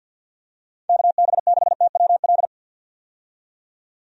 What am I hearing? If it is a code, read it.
Morse code, heard as a keyed tone, about 51 wpm, 700 Hz. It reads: KB6TWL
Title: USA Call Signs - 6 Characters - 50wpm